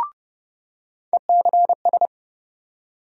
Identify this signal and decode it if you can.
Morse code, heard as a keyed tone, about 30 wpm, 700 Hz. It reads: ECS
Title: USA Call Sign Postfixes - 3 Letters 30wpm